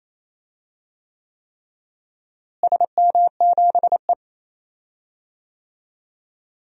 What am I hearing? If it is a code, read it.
Morse code, heard as a keyed tone, about 28 wpm, 700 Hz. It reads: SM7E
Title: DXCC-100 Call Signs - 4 Characters - Repeated 3x - 28wpm